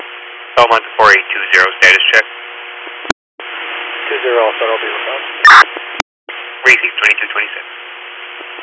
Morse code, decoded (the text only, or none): none